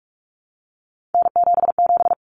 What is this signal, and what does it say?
Morse code, heard as a keyed tone, about 44 wpm, 700 Hz. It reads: N77
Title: Common Cut Numbers - 3-Digit 45wpm